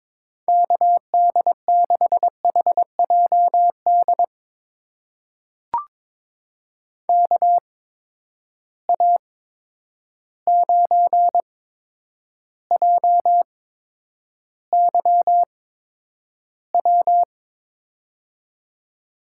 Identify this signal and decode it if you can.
Morse code, heard as a keyed tone, about 22 wpm, spaced as for 9 wpm, 700 Hz. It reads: KD6HJD KA9JYW
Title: USA Call Signs - 8x Character Spacing - 22wpm